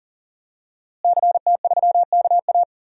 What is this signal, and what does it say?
Morse code, heard as a keyed tone, about 40 wpm, 700 Hz. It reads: CT3KA
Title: DXCC-100 Call Signs - 5 Characters - Repeated 3x - 40wpm